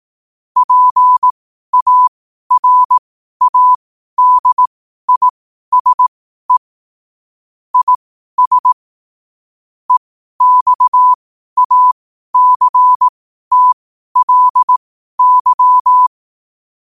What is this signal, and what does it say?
Morse code, heard as a keyed tone, about 18 wpm, spaced as for 12 wpm, 1000 Hz. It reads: PARADISE IS EXACTLY
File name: qotdfiles / 12wpm